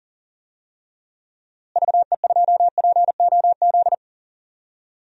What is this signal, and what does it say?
Morse code, heard as a keyed tone, about 40 wpm, 700 Hz. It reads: VE2POZ
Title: DXCC-100 Call Signs - 6 Characters - Repeated 2x - 40wpm